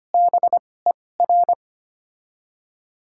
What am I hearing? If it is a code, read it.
Morse code, heard as a keyed tone, about 25 wpm, spaced as for 18 wpm, 690 Hz. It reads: BER
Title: 2x Character Spacing - Top 100 Words - 25wpm